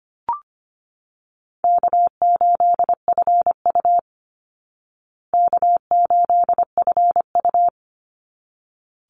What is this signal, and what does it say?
Morse code, heard as a keyed tone, about 25 wpm, 700 Hz. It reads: K8FU K8FU
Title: Easy Call Signs Repeated 2x - 25wpm